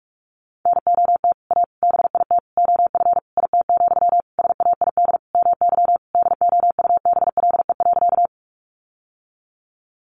Morse code, heard as a keyed tone, about 44 wpm, 700 Hz. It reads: NOT A BIT OF IT, SAID MY DOUBLE.
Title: Sentences from Top 1400 Words 45wpm